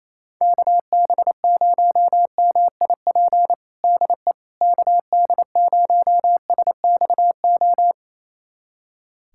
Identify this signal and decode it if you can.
Morse code, heard as a keyed tone, about 28 wpm, 700 Hz. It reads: KB0MIP DE KD0SXO
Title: USA Callsign DE Callsign 28wpm